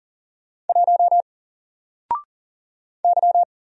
Morse code, heard as a keyed tone, about 40 wpm, 700 Hz. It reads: 1 Y
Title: Single Letter-Number 40wpm